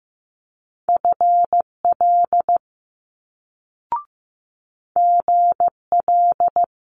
Morse code, heard as a keyed tone, about 15 wpm, 700 Hz. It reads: FL GL
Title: Multi-letter Phonemes Quick Fire 15wpm